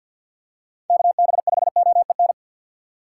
Morse code, heard as a keyed tone, about 48 wpm, 700 Hz. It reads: KB5OEN
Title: USA Call Signs - 6 Characters - 6x Character Spacing - 50wpm